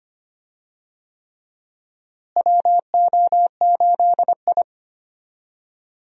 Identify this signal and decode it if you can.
Morse code, heard as a keyed tone, about 25 wpm, 700 Hz. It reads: WO8I
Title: USA Call Signs - 4 Characters - Repeated 3x - 25wpm